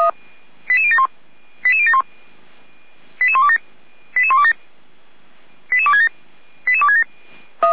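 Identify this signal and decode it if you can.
Five-tone selective call (ZVEI), heard as quick runs of five tones, tones of about 70 ms. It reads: 88071 88071 88127 88127 88267 88267
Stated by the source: ZVEI five-tone call received by SDR, fire brigade Feuerwehr Orlamünde.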